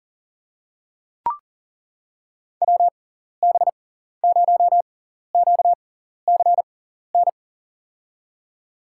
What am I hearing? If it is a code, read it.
Morse code, heard as a keyed tone, about 39 wpm, spaced as for 14 wpm, 700 Hz. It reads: WB0QCN
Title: USA Call Signs - 6 Characters - 6x Character Spacing - 40wpm